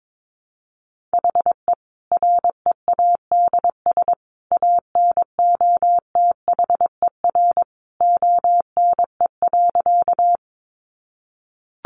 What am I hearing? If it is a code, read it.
Morse code, heard as a keyed tone, about 22 wpm, 700 Hz. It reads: HE READS ANOTHER ONE.
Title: Additional Easy Phrases 22wpm